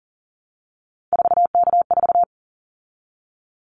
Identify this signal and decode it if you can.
Morse code, heard as a keyed tone, about 40 wpm, 700 Hz. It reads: 4K4
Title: Callsign Prefixes for Top 100 DXCC - 40wpm